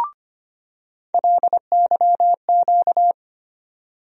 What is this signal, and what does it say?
Morse code, heard as a keyed tone, about 25 wpm, 700 Hz. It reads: LYQ
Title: USA Call Sign Postfixes - 3 Letters 25wpm